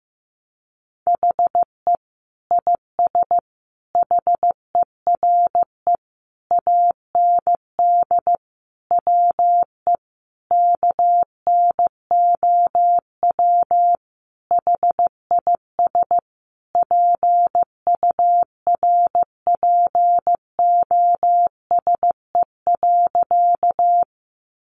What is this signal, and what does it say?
Morse code, heard as a keyed tone, about 15 wpm, 700 Hz. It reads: HE IS HERE AND WE KNOW HIS PURPOSE.
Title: Sentences from Top 800 Words 15wpm